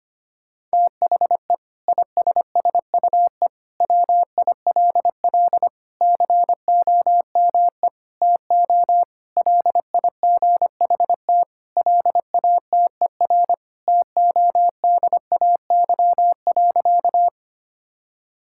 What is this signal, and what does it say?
Morse code, heard as a keyed tone, about 25 wpm, 700 Hz. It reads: THE ISSUE WILL COME TO LIGHT LATER TODAY.
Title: Sentences from Top 1000 Words - Review Part 6 - 25wpm